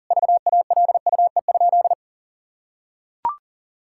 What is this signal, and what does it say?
Morse code, heard as a keyed tone, about 40 wpm, 700 Hz. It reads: VALUE?
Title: Sentences from Top 1000 Words - Review Part 6 - 40wpm